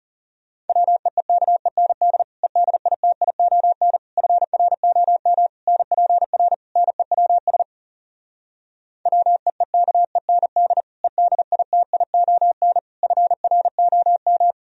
Morse code, heard as a keyed tone, about 34 wpm, 700 Hz. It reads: WEEKEND EDITION FROM NPR NEWS WEEKEND EDITION FROM
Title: Speed-Racing - Sets of 5 Words 30wpm